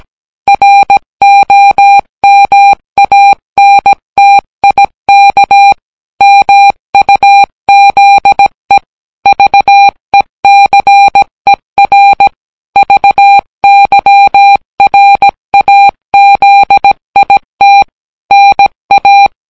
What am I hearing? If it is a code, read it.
Morse code, heard as a keyed tone, about 17 wpm, 800 Hz. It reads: ROMANTIK MUZE VECER VYRAZIT NA